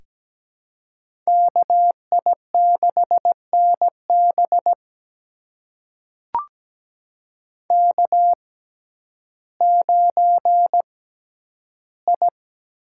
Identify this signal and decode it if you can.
Morse code, heard as a keyed tone, about 17 wpm, spaced as for 10 wpm, 700 Hz. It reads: KI6NB K9I
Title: USA Call Signs - 5 Characters - 6x Character Spacing - 17wpm